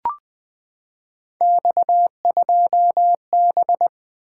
Morse code, heard as a keyed tone, about 20 wpm, 700 Hz. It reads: X2B
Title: Three Letter-Number Combinations - 20wpm